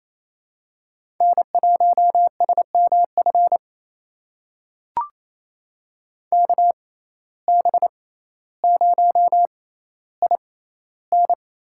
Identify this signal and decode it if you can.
Morse code, heard as a keyed tone, about 28 wpm, spaced as for 14 wpm, 700 Hz. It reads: N1SMF KB0IN
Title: USA Call Signs - 6x Character Spacing - 28wpm